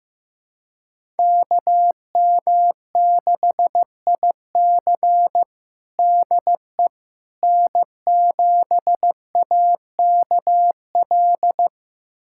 Morse code, heard as a keyed tone, about 16 wpm, 700 Hz. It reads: KM6IC DE N7AKL
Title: USA Callsign DE Callsign - 5 Characters - 15wpm